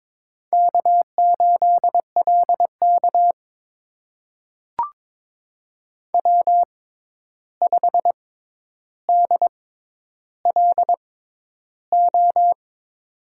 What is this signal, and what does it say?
Morse code, heard as a keyed tone, about 22 wpm, spaced as for 11 wpm, 700 Hz. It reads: K8LK W5DLO
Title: USA Call Signs - 6x Character Spacing - 22wpm